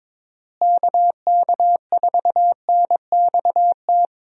Morse code, heard as a keyed tone, about 21 wpm, 700 Hz. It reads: KK4NXT
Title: USA Call Signs - 6 Characters - Repeated 2x - 22wpm